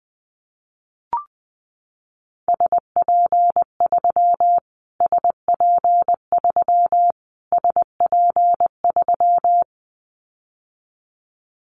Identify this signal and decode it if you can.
Morse code, heard as a keyed tone, about 20 wpm, 700 Hz. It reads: SP3 SP3 SP3
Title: Callsign Prefixes for Top 25 DXCC - Repeated 3x - 20wpm